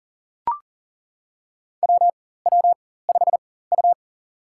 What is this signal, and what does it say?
Morse code, heard as a keyed tone, about 40 wpm, spaced as for 19 wpm, 700 Hz. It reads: WW5U
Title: USA Call Signs - 4 Characters - 4x Character Spacing - 40wpm